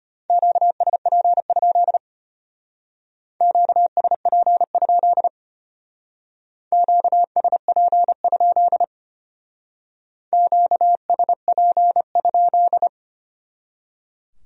Morse code, heard as a keyed tone, about 32 wpm, 700 Hz. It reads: QSP? QSP? QSP? QSP?